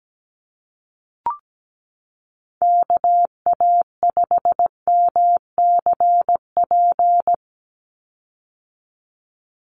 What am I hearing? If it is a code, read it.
Morse code, heard as a keyed tone, about 17 wpm, 700 Hz. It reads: KA5MCP
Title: USA Call Signs 17wpm